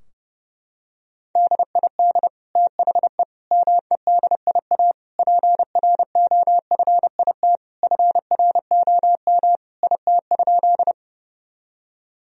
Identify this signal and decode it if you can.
Morse code, heard as a keyed tone, about 30 wpm, 700 Hz. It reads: DID THE MEDIA PROFIT FROM IT?
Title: Sentences from Top 1000 Words - Review Part 4 - 30wpm